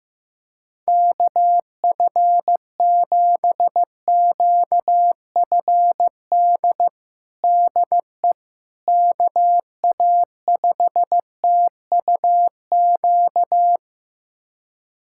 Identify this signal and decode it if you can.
Morse code, heard as a keyed tone, about 15 wpm, 700 Hz. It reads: KF7QFD DE KA5TUQ
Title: USA Callsign DE Callsign - 6 Characters - 15wpm